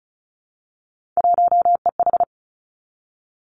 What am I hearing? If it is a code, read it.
Morse code, heard as a keyed tone, about 36 wpm, 700 Hz. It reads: 1EH
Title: Three Challenging Letter-Number Combinations 35wpm